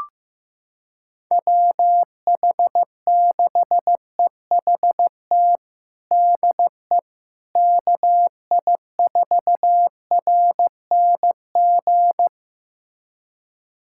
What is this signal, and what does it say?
Morse code, heard as a keyed tone, about 15 wpm, 700 Hz. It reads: WH6EHT DE KI4RNG
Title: USA Callsign DE Callsign 15wpm